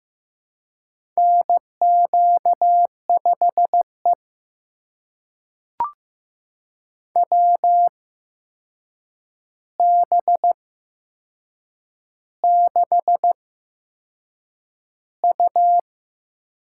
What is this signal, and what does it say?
Morse code, heard as a keyed tone, about 15 wpm, spaced as for 6 wpm, 700 Hz. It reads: NQ5E WB6U
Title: USA Call Signs - 8x Character Spacing - 15wpm